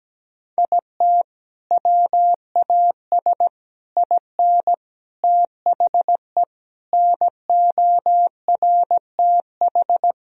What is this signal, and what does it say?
Morse code, heard as a keyed tone, about 17 wpm, 700 Hz. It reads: IT WAS IN THE NORTH
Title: Sentences from Top 1000 Words - Review Part 1 - 17wpm